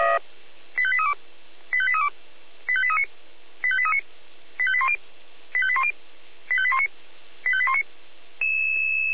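Five-tone selective call (ZVEI), heard as quick runs of five tones, tones of about 70 ms. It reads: 85832 85832 85839 85839 85810 85810 85819 85819
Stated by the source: ZVEI five-tone selective call received on an SDR.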